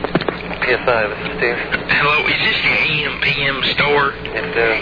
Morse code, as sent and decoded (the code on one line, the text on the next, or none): none